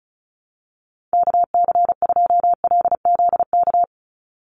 Morse code, heard as a keyed tone, about 34 wpm, 700 Hz. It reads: KC2LZK
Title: USA Call Signs 35wpm